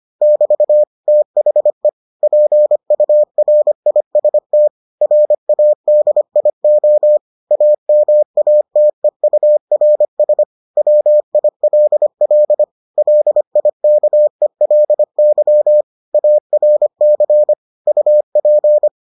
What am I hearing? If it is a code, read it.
Morse code, heard as a keyed tone, about 25 wpm, 600 Hz. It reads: <BT> THE PURIST RADIO AMATEURS WILL LIKELY ARC UP